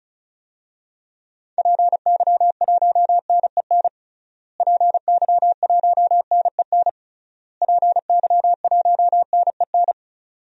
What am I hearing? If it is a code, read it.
Morse code, heard as a keyed tone, about 36 wpm, 700 Hz. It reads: PY1NEN PY1NEN PY1NEN
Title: DXCC-100 Call Signs - 6 Characters - Repeated 3x - 35wpm